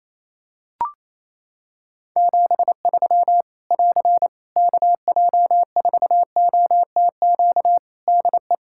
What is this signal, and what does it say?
Morse code, heard as a keyed tone, about 27 wpm, 700 Hz. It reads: 73 <AR> KJ4OTQ DE
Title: Practice QSO - Exchange 4 - Ending - 28wpm